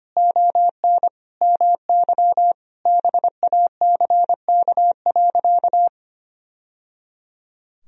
Morse code, heard as a keyed tone, about 25 wpm, 700 Hz. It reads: ON MY BACK.
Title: Sentences from Top 1000 Words - Review Part 1 - 25wpm